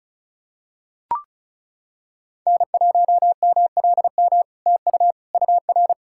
Morse code, heard as a keyed tone, about 37 wpm, 700 Hz. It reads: N1MLM TU UR
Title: SOTA QSO Practice 35wpm